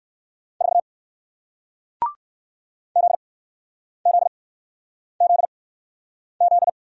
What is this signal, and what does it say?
Morse code, heard as a keyed tone, about 60 wpm, spaced as for 9 wpm, 700 Hz. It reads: 4 ZZZZ